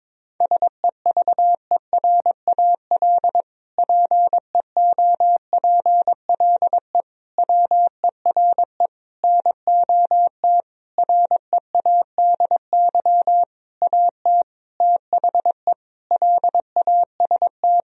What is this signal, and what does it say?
Morse code, heard as a keyed tone, about 22 wpm, 700 Hz. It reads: SEVERAL PEOPLE WERE NOT READY AT THE LAST
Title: Sentences from Top 1000 Words - Review Part 2 - 22wpm